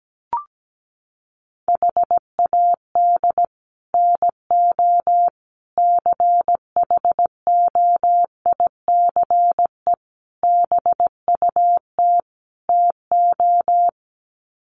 Morse code, heard as a keyed tone, about 17 wpm, 700 Hz. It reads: HAD NO CHOICE BUT TO
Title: Sets of 5 Words 17wpm